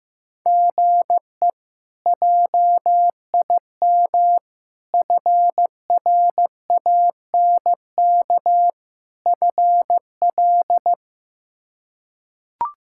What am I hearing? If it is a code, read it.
Morse code, heard as a keyed tone, about 16 wpm, 700 Hz. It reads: GE JIM FRANK FL
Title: K1USN SST QSO Practice - 15wpm